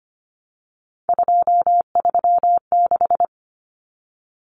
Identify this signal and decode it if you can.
Morse code, heard as a keyed tone, about 25 wpm, 700 Hz. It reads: 236
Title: Numbers - 3-Digit 25wpm